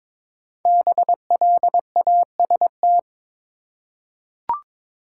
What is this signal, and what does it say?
Morse code, heard as a keyed tone, about 22 wpm, 700 Hz. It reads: BLAST
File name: 5 Letter Words - First 3 Letters Given - 22wpm